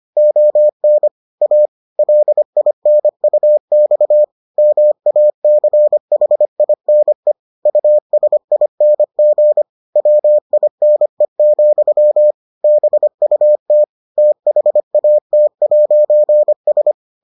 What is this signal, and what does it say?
Morse code, heard as a keyed tone, about 25 wpm, 600 Hz. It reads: ON A LINUX MACHINE USING WINE, BUT THAT'S